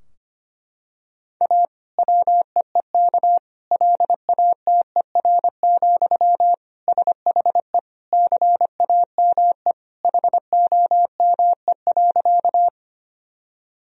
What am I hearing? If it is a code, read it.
Morse code, heard as a keyed tone, about 25 wpm, 700 Hz. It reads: A WEEK LATER, SHE CAME HOME.
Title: Sentences from Top 1000 Words - Review Part 4 - 25wpm